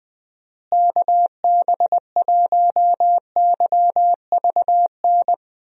Morse code, heard as a keyed tone, about 20 wpm, 700 Hz. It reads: KB1YVN
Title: USA Call Signs - 6 Characters - Repeated 2x - 20wpm